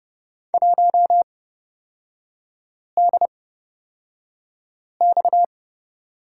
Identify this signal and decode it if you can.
Morse code, heard as a keyed tone, about 30 wpm, spaced as for 5 wpm, 700 Hz. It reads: 1DX
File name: Lesson 37e - Letter x - Three Characters 30wpm